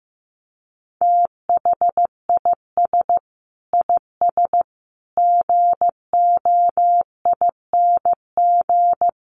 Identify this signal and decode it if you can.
Morse code, heard as a keyed tone, about 15 wpm, 700 Hz. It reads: THIS IS GOING
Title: Sets of 5 Words 15wpm